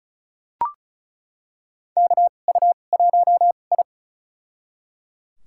Morse code, heard as a keyed tone, about 36 wpm, spaced as for 25 wpm, 700 Hz. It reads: KU1I
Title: USA Call Signs - 4 Characters - 2x Character Spacing - 35wpm